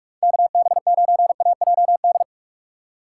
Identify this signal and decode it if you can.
Morse code, heard as a keyed tone, about 45 wpm, 700 Hz. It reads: KB9AJD
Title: USA Call Signs - 3x Character Spacing - 45wpm